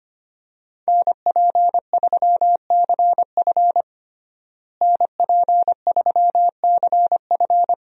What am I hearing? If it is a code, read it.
Morse code, heard as a keyed tone, about 25 wpm, 700 Hz. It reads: NP3CF NP3CF
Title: USA Call Signs - 5 Characters - Repeated 2x - 25wpm